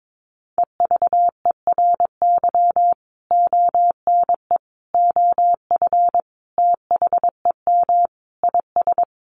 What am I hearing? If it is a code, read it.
Morse code, heard as a keyed tone, about 22 wpm, 700 Hz. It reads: EVERY ONE OF THEM IS
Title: Sentences from Top 1100 Words 22wpm